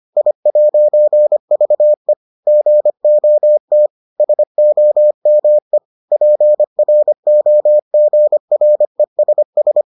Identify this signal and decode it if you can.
Morse code, heard as a keyed tone, about 25 wpm, 600 Hz. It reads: I'VE GOT SOME PROGRESS